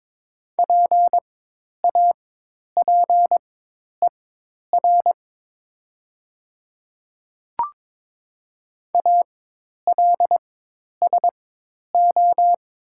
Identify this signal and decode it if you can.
Morse code, heard as a keyed tone, about 22 wpm, spaced as for 10 wpm, 700 Hz. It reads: PAPER ALSO
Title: 4x Character Spacing - Top 300 Words - 22wpm